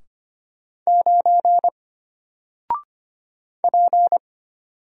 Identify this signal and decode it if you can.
Morse code, heard as a keyed tone, about 25 wpm, 700 Hz. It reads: 9 P
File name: Single-Letter-Number-25wpm